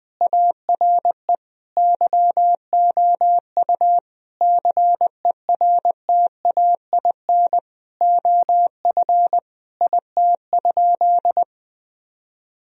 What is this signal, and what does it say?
Morse code, heard as a keyed tone, about 20 wpm, 700 Hz. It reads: ARE YOU CERTAIN OF IT?
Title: Sentences from Top 1000 Words - Review Part 3 - 20wpm